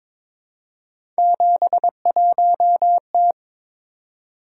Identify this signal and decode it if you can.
Morse code, heard as a keyed tone, about 23 wpm, 700 Hz. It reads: 71T
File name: practice / 22wpm / Three Letter-Number Combinations - 22wpm